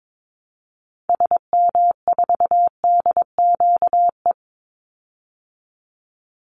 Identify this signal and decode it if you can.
Morse code, heard as a keyed tone, about 22 wpm, 690 Hz. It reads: SM4DQE
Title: Call Signs Repeated 3x - 22wpm